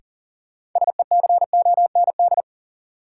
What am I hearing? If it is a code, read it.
Morse code, heard as a keyed tone, about 40 wpm, 700 Hz. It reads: SECOND
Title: Sets of 2 Words - New Word Drill - 40wpm